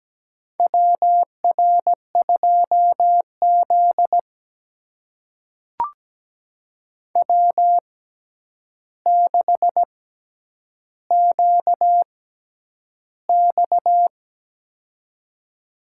Morse code, heard as a keyed tone, about 17 wpm, spaced as for 9 wpm, 700 Hz. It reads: WR2Z W6QX
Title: USA Call Signs - 4 Characters - 6x Character Spacing - 17wpm